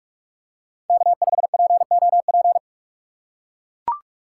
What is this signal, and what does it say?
Morse code, heard as a keyed tone, about 42 wpm, 700 Hz. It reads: K5POP
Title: USA Call Signs - 5 Characters - 8x Character Spacing - 45wpm